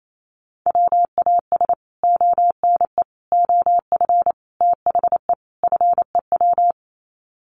Morse code, heard as a keyed tone, about 28 wpm, 700 Hz. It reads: WAS ONE OF THE FEW
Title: Sets of 5 Words Encore 28wpm